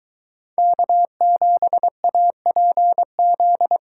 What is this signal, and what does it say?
Morse code, heard as a keyed tone, about 22 wpm, 700 Hz. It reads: K7APZ